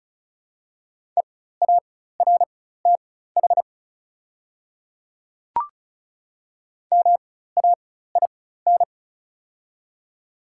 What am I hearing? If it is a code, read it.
Morse code, heard as a keyed tone, about 36 wpm, spaced as for 16 wpm, 690 Hz. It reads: EARTH MAIN
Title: 4x Character Spacing - Top 300 Words - 35wpm